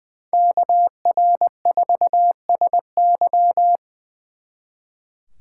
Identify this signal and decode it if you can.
Morse code, heard as a keyed tone, about 24 wpm, 700 Hz. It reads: KR4SY
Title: Speed Racing - USA Call Signs 20wpm